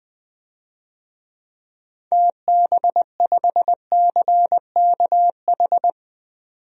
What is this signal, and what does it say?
Morse code, heard as a keyed tone, about 20 wpm, 700 Hz. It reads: TB5CKH
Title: DXCC-100 Call Signs - 6 Characters - 20wpm